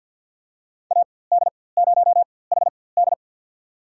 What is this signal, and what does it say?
Morse code, heard as a keyed tone, about 53 wpm, spaced as for 24 wpm, 700 Hz. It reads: AD0HD
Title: USA Call Signs - 5 Characters - 4x Character Spacing - 50wpm